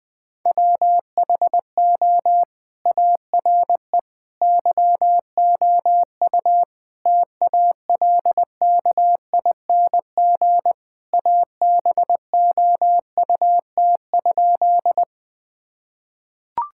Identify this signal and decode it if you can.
Morse code, heard as a keyed tone, about 19 wpm, 700 Hz. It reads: WHO ARE YOU TALKING ABOUT?
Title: Sentences with 2-Word Phrases - 20wpm